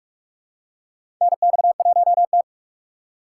Morse code, heard as a keyed tone, about 47 wpm, 700 Hz. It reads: NX1T
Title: USA Call Signs - 4 Characters - 4x Character Spacing - 45wpm